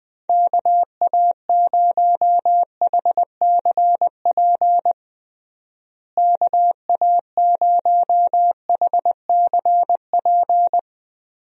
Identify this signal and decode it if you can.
Morse code, heard as a keyed tone, about 20 wpm, 700 Hz. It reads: KA0HCP KA0HCP
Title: USA Call Signs - 6 Characters - Repeated 3x - 20wpm